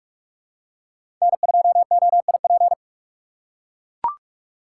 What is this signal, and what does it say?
Morse code, heard as a keyed tone, about 41 wpm, 700 Hz. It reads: N2OIP
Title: USA Call Signs - 3x Character Spacing - 45wpm